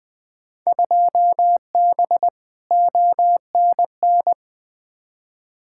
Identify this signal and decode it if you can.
Morse code, heard as a keyed tone, about 21 wpm, 700 Hz. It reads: 2B ONN
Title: Field Day Exchanges Encore - 20wpm